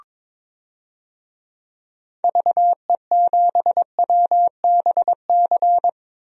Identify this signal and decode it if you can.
Morse code, heard as a keyed tone, about 22 wpm, 700 Hz. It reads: VE7WBC
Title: DXCC-100 Call Signs - 6 Characters - 22wpm